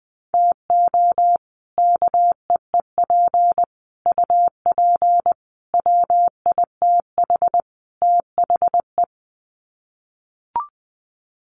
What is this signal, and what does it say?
Morse code, heard as a keyed tone, about 19 wpm, 700 Hz. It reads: TO KEEP UP WITH THE
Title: Sets of 5 Words Encore 20wpm